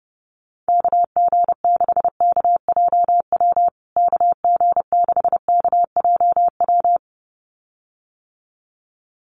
Morse code, heard as a keyed tone, about 30 wpm, 700 Hz. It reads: KG6KJW KG6KJW
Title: USA Call Signs Repeated 2x 30wpm